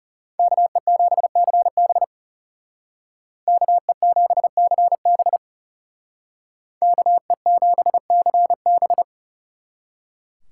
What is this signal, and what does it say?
Morse code, heard as a keyed tone, about 36 wpm, 700 Hz. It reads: KE7CB KE7CB KE7CB
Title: Speed Racing - USA Call Signs 30wpm